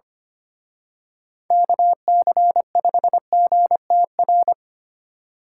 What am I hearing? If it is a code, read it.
Morse code, heard as a keyed tone, about 29 wpm, 700 Hz. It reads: KC5GTR
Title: Speed Racing - USA 2x3 Call Signs 25wpm